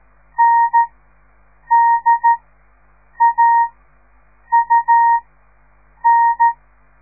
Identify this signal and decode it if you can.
Morse code, heard as a keyed tone, about 13 wpm, spaced as for 7 wpm, 930 Hz. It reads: NDAUN
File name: radiomorningstar / morse / les 4